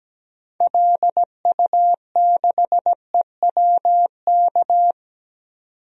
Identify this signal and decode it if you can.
Morse code, heard as a keyed tone, about 17 wpm, 700 Hz. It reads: LU6EWK
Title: DXCC-100 Call Signs - 6 Characters - 17wpm